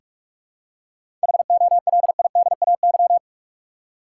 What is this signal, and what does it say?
Morse code, heard as a keyed tone, about 42 wpm, 700 Hz. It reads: HOLIDAY
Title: Seven Letter Words - 45wpm